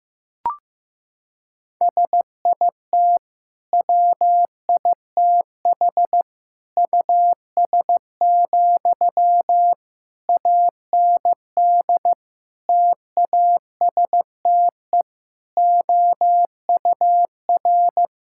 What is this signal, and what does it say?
Morse code, heard as a keyed tone, about 15 wpm, 700 Hz. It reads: SIT WITH US, AND TASTE OUR
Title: Sentences from Top 1600 Words 15wpm